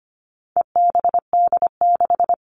Morse code, heard as a keyed tone, about 25 wpm, 700 Hz. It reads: EBD6
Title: All Cut Numbers - 4-Digit 25wpm